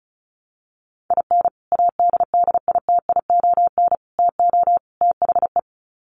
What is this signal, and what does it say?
Morse code, heard as a keyed tone, about 34 wpm, 700 Hz. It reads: IN ADDITION TO THE
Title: Sets of 4 Words 35wpm